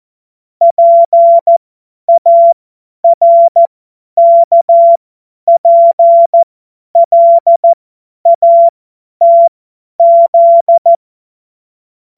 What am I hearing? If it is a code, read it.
Morse code, heard as a keyed tone, about 14 wpm, spaced as for 10 wpm, 680 Hz. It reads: PARKPLATZ
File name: Morsekurs / Witze / Witz 01. Juni 2021, 50 F70